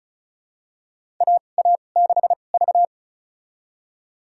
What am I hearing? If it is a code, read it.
Morse code, heard as a keyed tone, about 35 wpm, spaced as for 25 wpm, 700 Hz. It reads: AA6V
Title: USA Call Signs - 4 Characters - 2x Character Spacing - 35wpm